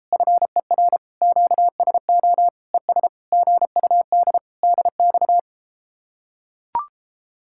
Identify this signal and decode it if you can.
Morse code, heard as a keyed tone, about 34 wpm, 700 Hz. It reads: FER QSO ES GUD DX